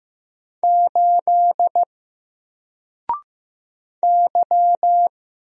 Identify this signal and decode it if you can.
Morse code, heard as a keyed tone, about 15 wpm, 700 Hz. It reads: 8 Y
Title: Single Letter-Number 15wpm